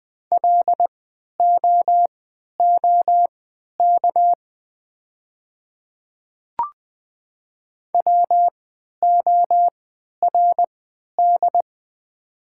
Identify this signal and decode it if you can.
Morse code, heard as a keyed tone, about 21 wpm, spaced as for 12 wpm, 700 Hz. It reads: LOOK WORD
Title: 3x Character Spacing - Top 100 Words - 20wpm